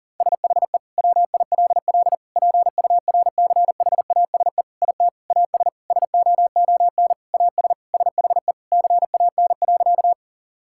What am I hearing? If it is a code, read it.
Morse code, heard as a keyed tone, about 40 wpm, 700 Hz. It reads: SHE WILL PURCHASE IT AS SOON AS SHE CAN.
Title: Sentences from Top 1000 Words - Review Part 5 - 40wpm